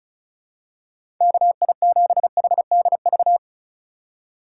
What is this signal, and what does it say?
Morse code, heard as a keyed tone, about 36 wpm, 700 Hz. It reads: KI7HDV
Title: USA Call Signs - 6 Characters - 8x Character Spacing - 35wpm